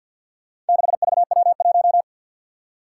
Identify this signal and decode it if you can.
Morse code, heard as a keyed tone, about 47 wpm, 700 Hz. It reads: 6VW1
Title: Four Challenging Letter-Number Combinations 50wpm